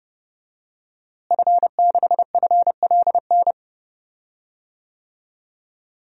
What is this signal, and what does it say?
Morse code, heard as a keyed tone, about 32 wpm, 700 Hz. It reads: F6FLN
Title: DXCC-100 Call Signs - 5 Characters - Repeated 3x - 30wpm